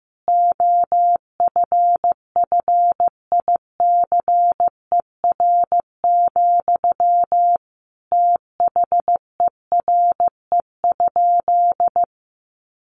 Morse code, heard as a keyed tone, about 15 wpm, 700 Hz. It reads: OFFICER, THERE?
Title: Sentences from Top 1200 Words 15wpm